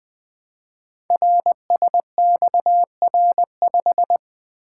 Morse code, heard as a keyed tone, about 21 wpm, 700 Hz. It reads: RSXR5
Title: Code Groups - 5 Letter:Numbers - 20wpm